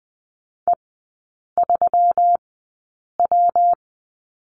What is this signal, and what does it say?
Morse code, heard as a keyed tone, about 20 wpm, spaced as for 8 wpm, 700 Hz. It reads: E3W
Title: 133 Lesson 22e - Question Mark - Three Characters 20wpm